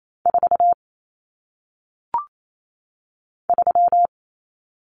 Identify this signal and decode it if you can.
Morse code, heard as a keyed tone, about 28 wpm, 700 Hz. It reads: 4 3
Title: Numbers - 1-Digit 28wpm